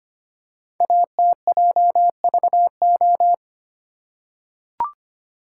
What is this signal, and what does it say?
Morse code, heard as a keyed tone, about 25 wpm, 700 Hz. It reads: ATJVO
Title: Code Groups - 5 Letters - 25wpm